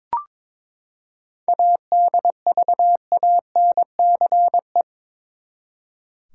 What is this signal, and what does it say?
Morse code, heard as a keyed tone, about 21 wpm, 700 Hz. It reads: ADVANCE
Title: Seven Letter Words - 22wpm